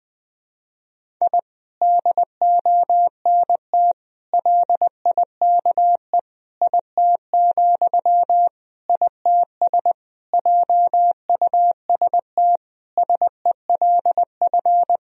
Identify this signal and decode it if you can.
Morse code, heard as a keyed tone, about 21 wpm, 700 Hz. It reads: I DONT LIKE IT, ITS JUST SELF
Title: Sentences from Top 1000 Words - Review Part 1 - 20wpm